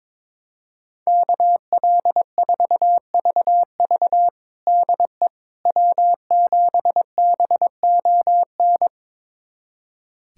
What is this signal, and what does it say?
Morse code, heard as a keyed tone, about 22 wpm, 700 Hz. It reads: KL4VV DE W7BON
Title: USA Callsign DE Callsign - 5 Characters - 22wpm